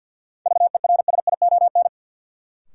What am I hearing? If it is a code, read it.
Morse code, heard as a keyed tone, about 53 wpm, 700 Hz. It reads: VERSION
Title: Sentences from Top 1400 Words - New Word Drill - 50wpm